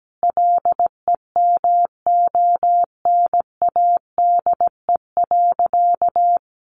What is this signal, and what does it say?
Morse code, heard as a keyed tone, about 17 wpm, 700 Hz. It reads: LEMONADE.
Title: Additional Easy Phrases 17wpm